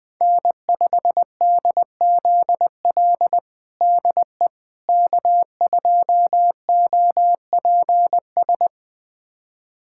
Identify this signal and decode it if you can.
Morse code, heard as a keyed tone, about 21 wpm, 700 Hz. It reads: N5DZL DE K2OPS
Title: USA Callsign DE Callsign - 5 Characters - 20wpm